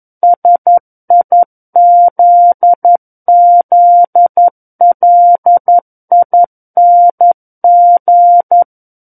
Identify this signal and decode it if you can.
Morse code, heard as a keyed tone, about 11 wpm, 700 Hz. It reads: SIZZLING
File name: Book One / 02-Chapter 2 - 11 WPM